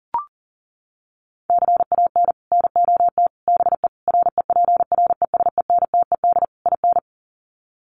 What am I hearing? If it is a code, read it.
Morse code, heard as a keyed tone, about 37 wpm, 700 Hz. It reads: CAN NOT BE REPRESENTED IN
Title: Sets of 5 Words 40wpm